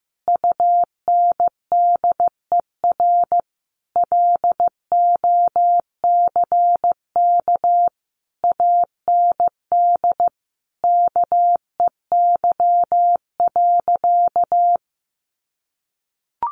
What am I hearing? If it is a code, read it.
Morse code, heard as a keyed tone, about 15 wpm, 700 Hz. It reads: UNDER LOCK AND KEY.
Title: Sentences from Top 1400 Words 15wpm